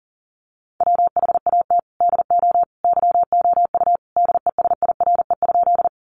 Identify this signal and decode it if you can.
Morse code, heard as a keyed tone, about 40 wpm, 700 Hz. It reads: WHAT DO YOU DESIRE?
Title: Sentences from Top 1100 Words 40wpm